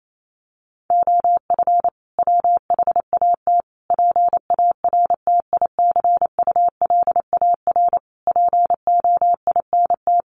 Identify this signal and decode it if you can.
Morse code, heard as a keyed tone, about 28 wpm, 700 Hz. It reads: OF WHAT PARTICULAR POINT
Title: Sentences from Top 1300 Words 28wpm